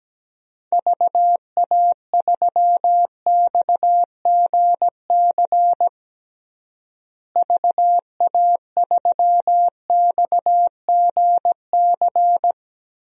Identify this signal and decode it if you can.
Morse code, heard as a keyed tone, about 17 wpm, 700 Hz. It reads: VA3XGC VA3XGC
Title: DXCC-100 Call Signs - 6 Characters - Repeated 3x - 17wpm